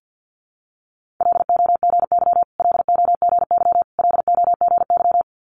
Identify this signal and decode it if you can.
Morse code, heard as a keyed tone, about 50 wpm, 690 Hz. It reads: LOGY LOGY LOGY
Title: Multi-letter Phonemes Repeated 3x 50wpm